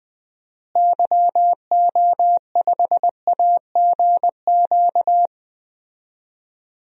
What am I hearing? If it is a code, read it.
Morse code, heard as a keyed tone, about 20 wpm, 700 Hz. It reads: YO5AGQ
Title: DXCC-100 Call Signs - 6 Characters - Repeated 2x - 20wpm